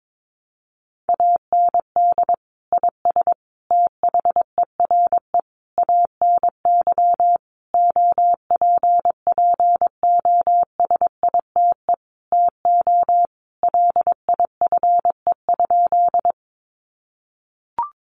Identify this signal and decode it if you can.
Morse code, heard as a keyed tone, about 22 wpm, 700 Hz. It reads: AND IS THERE ANY OPPOSITE TO LIFE?
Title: Sentences from Top 1300 Words 22wpm